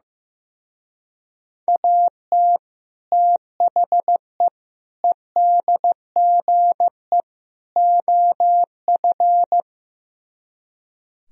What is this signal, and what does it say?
Morse code, heard as a keyed tone, about 16 wpm, 700 Hz. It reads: AT THE EDGE OF
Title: Speed-Racing - Sets of 4 Words 15wpm